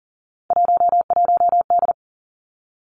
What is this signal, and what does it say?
Morse code, heard as a keyed tone, about 38 wpm, 700 Hz. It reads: 11D
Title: Three Challenging Letter-Number Combinations 40wpm